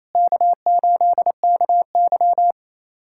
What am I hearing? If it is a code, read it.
Morse code, heard as a keyed tone, about 28 wpm, 700 Hz. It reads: K8KY
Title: USA Callsign DE Callsign 28wpm